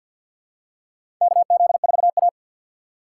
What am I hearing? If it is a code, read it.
Morse code, heard as a keyed tone, about 52 wpm, 700 Hz. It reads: KZ4A
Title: K1USN SST QSO Practice - 50wpm